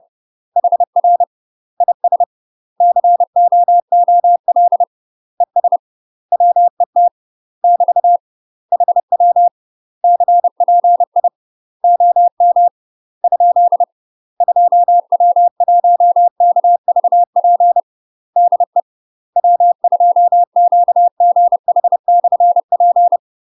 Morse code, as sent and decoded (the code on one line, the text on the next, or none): .... .-. / .. ... / -.-. --- --- .-.. / . ... / .-- . - / -...- / .... .-- / -.-. .--. .. / --- -- / ..--.. / ..--- .-- .---- -.- ...- .--. / -.. / .-- ..--- --.- --. .... -..-. .--.
HR IS COOL ES WET <BT> HW CPI OM ? 2W1KVP DE W2QGH/P